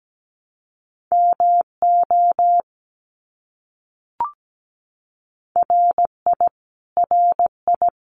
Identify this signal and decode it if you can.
Morse code, heard as a keyed tone, about 17 wpm, 700 Hz. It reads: MO RI RI
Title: US States Repeated 3x 17wpm